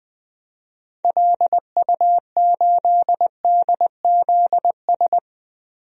Morse code, heard as a keyed tone, about 20 wpm, 700 Hz. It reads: LU8DZS
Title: DXCC-100 Call Signs - 6 Characters - 20wpm